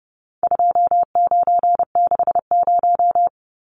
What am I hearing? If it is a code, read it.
Morse code, heard as a keyed tone, about 29 wpm, 700 Hz. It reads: 2960
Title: Numbers - 4-Digit 30wpm